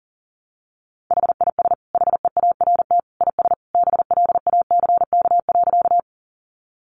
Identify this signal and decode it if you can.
Morse code, heard as a keyed tone, about 40 wpm, 700 Hz. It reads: HIS HEART IS BLACK.
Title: Sentences from Top 1200 Words 40wpm